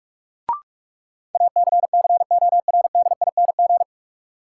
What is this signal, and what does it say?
Morse code, heard as a keyed tone, about 41 wpm, 700 Hz. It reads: ACCORDING
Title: Sets of 2 Words - New Word Drill - 45wpm